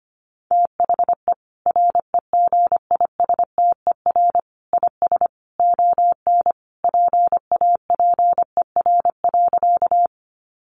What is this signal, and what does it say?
Morse code, heard as a keyed tone, about 25 wpm, 700 Hz. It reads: THE REGISTER IS ON PAPER.
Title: Sentences from Top 1200 Words 25wpm